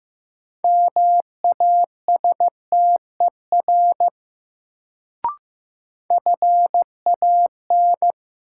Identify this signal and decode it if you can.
Morse code, heard as a keyed tone, about 15 wpm, 700 Hz. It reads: MASTER FAN
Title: Sentences from Top 1300 Words - New Word Drill Alt - 15wpm